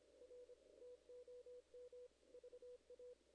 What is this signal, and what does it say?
Morse code, heard as a keyed tone, about 25 wpm, 490 Hz. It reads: R TOM VA
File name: 2016 JAN CW NAQP / 2016 Jan naqp k5ot - 0-7